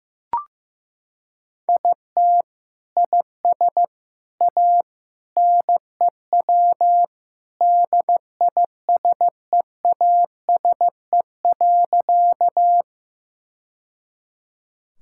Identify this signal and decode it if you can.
Morse code, heard as a keyed tone, about 15 wpm, 700 Hz. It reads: IT IS A NEW DISEASE.
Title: Sentences from Top 1000 Words - Review Part 6 - 15wpm